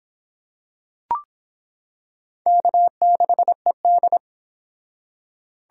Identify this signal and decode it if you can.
Morse code, heard as a keyed tone, about 26 wpm, 700 Hz. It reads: K6ED